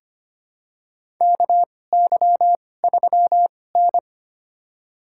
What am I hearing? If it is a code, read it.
Morse code, heard as a keyed tone, about 25 wpm, spaced as for 18 wpm, 700 Hz. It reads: KY3N
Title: USA Call Signs - 4 Characters - 2x Character Spacing - 25wpm